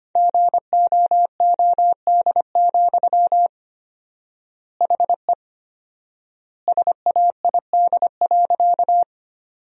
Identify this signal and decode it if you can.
Morse code, heard as a keyed tone, about 25 wpm, 700 Hz. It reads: GOOD, HE SAID.
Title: 4x Word-Spacing - Sentences from Top 200 Words - 25wpm